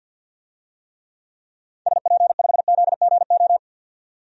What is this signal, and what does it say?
Morse code, heard as a keyed tone, about 48 wpm, 700 Hz. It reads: SP5ZGO
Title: DXCC-100 Call Signs - 6 Characters - 50wpm